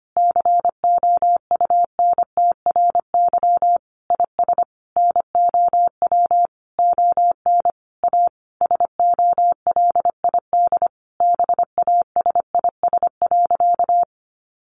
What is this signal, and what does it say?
Morse code, heard as a keyed tone, about 25 wpm, 700 Hz. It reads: COUNTRY IS NOW ON A SOLID BASIS.
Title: Sentences from Top 1500 Words 25wpm